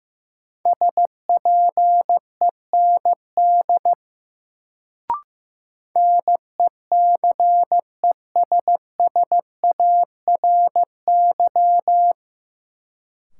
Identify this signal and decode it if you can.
Morse code, heard as a keyed tone, about 15 wpm, 700 Hz. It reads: SPEND NECESSARY
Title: Sentences from Top 800 Words - New Word Drill - 15wpm